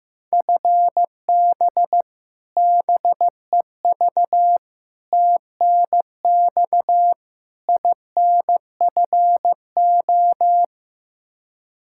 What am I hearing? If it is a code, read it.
Morse code, heard as a keyed tone, about 15 wpm, 700 Hz. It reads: FB BEV TNX INFO
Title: Practice QSO - Exchange 3 - More Info - 15wpm